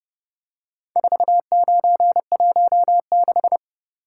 Morse code, heard as a keyed tone, about 29 wpm, 700 Hz. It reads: 4916